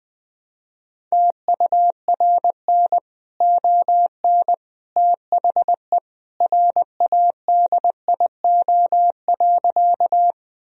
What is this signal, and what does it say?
Morse code, heard as a keyed tone, about 20 wpm, 700 Hz. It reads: TURN ON THE RADIO.
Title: Sentences from Top 1000 Words - Review Part 3 - 20wpm